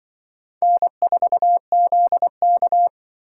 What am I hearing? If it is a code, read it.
Morse code, heard as a keyed tone, about 24 wpm, 700 Hz. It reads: N4ZK
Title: Speed Racing - USA 1x2 Call Signs 20wpm